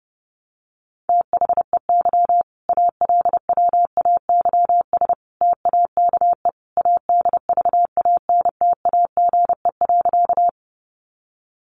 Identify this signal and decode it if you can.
Morse code, heard as a keyed tone, about 28 wpm, 700 Hz. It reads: THEY ALWAYS TAKE ADVANTAGE.
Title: Sentences from Top 900 Words 30wpm